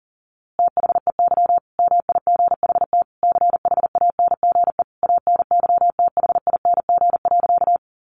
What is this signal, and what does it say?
Morse code, heard as a keyed tone, about 40 wpm, 700 Hz. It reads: THEY MIGHT CHANGE ANYTHING.
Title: A Sound Of Thunder 40wpm